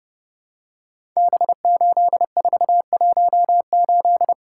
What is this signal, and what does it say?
Morse code, heard as a keyed tone, about 28 wpm, 700 Hz. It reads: B8418
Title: Code Groups - 5 Letter:Numbers - 30wpm